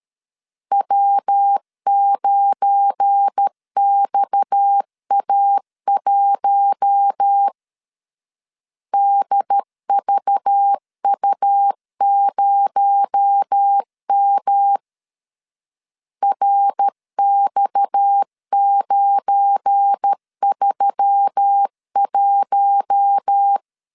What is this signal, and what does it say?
Morse code, heard as a keyed tone, about 13 wpm, 800 Hz. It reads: W9XA1 DVU0M RX931
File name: LEZIONI / 7 GF9Z / 7 GF9 12wpm 2319